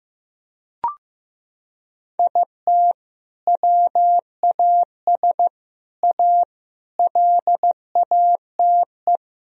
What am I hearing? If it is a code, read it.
Morse code, heard as a keyed tone, about 15 wpm, 700 Hz. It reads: IT WAS A LATE
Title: Sentences from Top 1000 Words - Review Part 4 - 15wpm